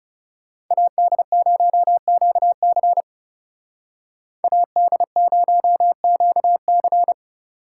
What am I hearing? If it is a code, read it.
Morse code, heard as a keyed tone, about 36 wpm, 700 Hz. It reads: AD0QC AD0QC
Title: Speed Racing - USA 2x2 Call Signs 30wpm